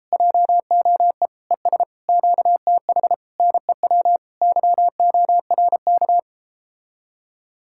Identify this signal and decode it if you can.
Morse code, heard as a keyed tone, about 38 wpm, 700 Hz. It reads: JOE ES QTH NEW YORK